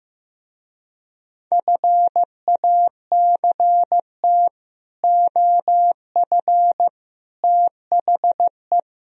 Morse code, heard as a keyed tone, about 16 wpm, 700 Hz. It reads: FACT OF THE
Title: Speed-Racing - Sets of 5 Words 15wpm